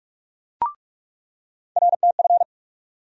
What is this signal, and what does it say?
Morse code, heard as a keyed tone, about 46 wpm, 700 Hz. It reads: RTF
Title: USA Call Sign Postfixes - 3 Letters 45wpm